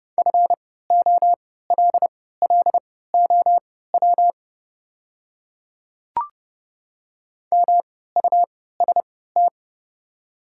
Morse code, heard as a keyed tone, about 30 wpm, spaced as for 17 wpm, 700 Hz. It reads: FOLLOW MUST
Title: 3x Character Spacing - Top 200 Words - 30wpm